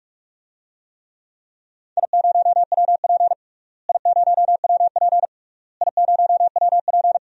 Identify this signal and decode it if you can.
Morse code, heard as a keyed tone, about 46 wpm, 700 Hz. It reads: I0WP I0WP I0WP
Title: DXCC-100 Call Signs - 4 Characters - Repeated 3x - 45wpm